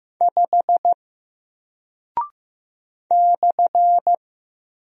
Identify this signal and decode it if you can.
Morse code, heard as a keyed tone, about 15 wpm, 700 Hz. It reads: 5 /
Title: Single Character (Letter:Number:Punctuation:Prosign) Alt 15wpm